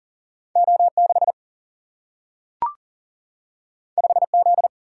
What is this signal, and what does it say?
Morse code, heard as a keyed tone, about 39 wpm, 700 Hz. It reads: O6 5Z
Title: Single-Letter Single-Number 40wpm